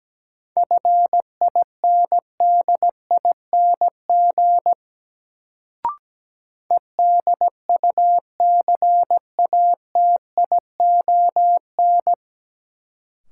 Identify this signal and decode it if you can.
Morse code, heard as a keyed tone, about 17 wpm, 700 Hz. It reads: FINDING EDUCATION
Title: Sentences from Top 1100 Words - New Word Drill Alt - 17wpm